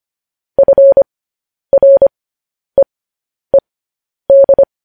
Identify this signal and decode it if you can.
Morse code, heard as a keyed tone, about 25 wpm, spaced as for 10 wpm, 550 Hz. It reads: FREED
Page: news headlines in morse code at 10 wpm